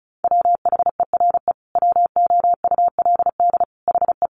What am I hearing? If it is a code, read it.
Morse code, heard as a keyed tone, about 34 wpm, 700 Hz. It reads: WHERE WOULD HE
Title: Sentences from Top 1300 Words 35wpm